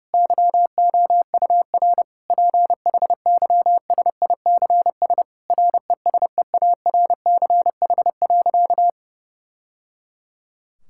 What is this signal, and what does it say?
Morse code, heard as a keyed tone, about 30 wpm, 700 Hz. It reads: YOUR PHYSICS RESEARCH.
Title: Sentences from Top 1000 Words - Review Part 5 - 30wpm